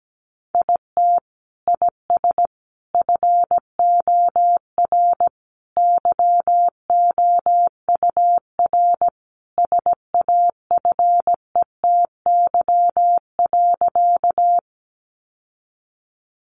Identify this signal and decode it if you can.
Morse code, heard as a keyed tone, about 17 wpm, 700 Hz. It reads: IT IS FOR YOUR SAFETY.
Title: Sentences from Top 700 Words 17wpm